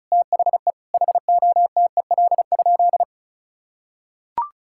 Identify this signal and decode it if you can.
Morse code, heard as a keyed tone, about 34 wpm, 700 Hz. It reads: THE HOTEL?
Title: Sentences from Top 1600 Words 35wpm